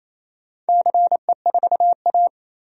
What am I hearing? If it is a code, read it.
Morse code, heard as a keyed tone, about 27 wpm, 690 Hz. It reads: CE4A
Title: DXCC-100 Call Signs - 4 Characters - 28wpm